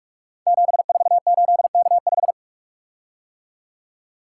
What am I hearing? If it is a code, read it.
Morse code, heard as a keyed tone, about 44 wpm, 700 Hz. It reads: 748K5
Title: Code Groups - 5 Letter:Numbers - 45wpm